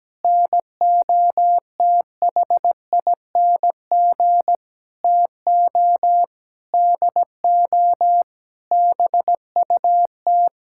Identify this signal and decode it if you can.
Morse code, heard as a keyed tone, about 17 wpm, 700 Hz. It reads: NOTHING TO DO BUT
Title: Sentences from Top 1000 Words - Review Part 6 - 17wpm